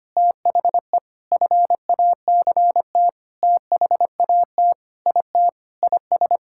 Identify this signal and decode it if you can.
Morse code, heard as a keyed tone, about 25 wpm, 700 Hz. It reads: THE FACT THAT IT IS
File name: practice / 25wpm / Sentences with 4-Word Phrases - 25wpm